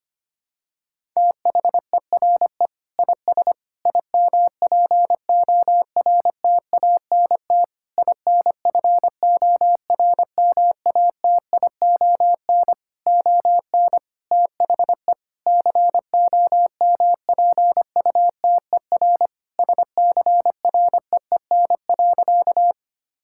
Morse code, heard as a keyed tone, about 25 wpm, 700 Hz. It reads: THERE IS IMPORTANT INFORMATION ON THE COMPUTER SCREEN.
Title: Sentences from Top 1000 Words - Review Part 6 - 25wpm